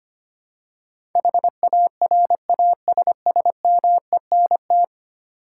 Morse code, heard as a keyed tone, about 25 wpm, 700 Hz. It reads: HARASSMENT
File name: 10 Letter Words - First 3 Letters Given - 25wpm